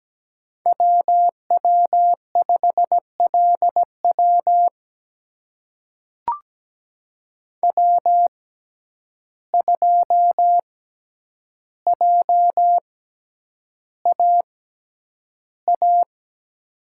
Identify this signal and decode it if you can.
Morse code, heard as a keyed tone, about 17 wpm, spaced as for 9 wpm, 700 Hz. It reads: WW5LW W2JAA
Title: USA Call Signs - 5 Characters - 6x Character Spacing - 17wpm